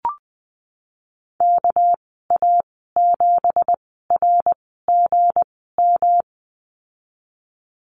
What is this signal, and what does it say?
Morse code, heard as a keyed tone, about 20 wpm, spaced as for 15 wpm, 700 Hz. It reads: KA7RGM
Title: 213 Lesson 34g - Number 8 - Call Signs 20wpm